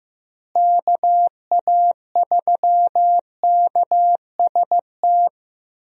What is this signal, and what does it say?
Morse code, heard as a keyed tone, about 15 wpm, 700 Hz. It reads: KA3KST
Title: USA Call Signs - 6 Characters - Repeated 3x - 15wpm